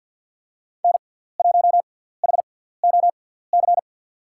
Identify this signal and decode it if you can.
Morse code, heard as a keyed tone, about 47 wpm, spaced as for 17 wpm, 700 Hz. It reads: N1HOC
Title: USA Call Signs - 6x Character Spacing - 50wpm